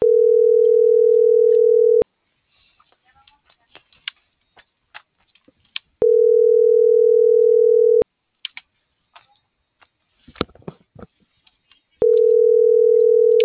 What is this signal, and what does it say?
Ringback tone (North America), 440 + 480 Hz.
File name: CENTRAL MORENA / out ram.lmorena.2025